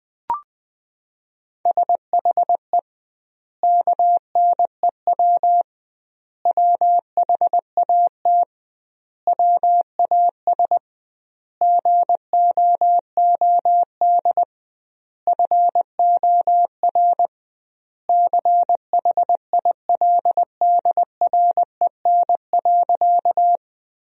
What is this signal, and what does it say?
Morse code, heard as a keyed tone, about 20 wpm, 700 Hz. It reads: SHE KNEW WHAT WAS GOOD FOR CHILDREN.
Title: 2x Word-Spacing - Sentences from Top 400 Words - 20wpm